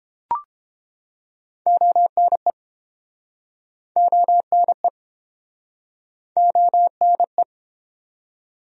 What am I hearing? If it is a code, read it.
Morse code, heard as a keyed tone, about 30 wpm, 700 Hz. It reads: ONE ONE ONE